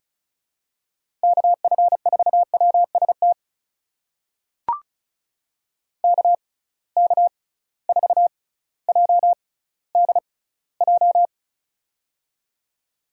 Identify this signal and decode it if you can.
Morse code, heard as a keyed tone, about 36 wpm, spaced as for 18 wpm, 700 Hz. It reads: KF4WST KK4JDJ
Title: USA Call Signs - 6 Characters - 6x Character Spacing - 35wpm